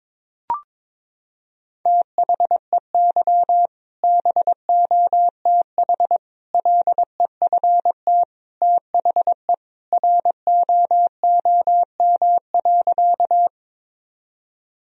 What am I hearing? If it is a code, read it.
Morse code, heard as a keyed tone, about 22 wpm, 700 Hz. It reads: THEY BOTH LEFT THE ROOM.
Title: Sentences from Top 1000 Words - Review Part 1 - 22wpm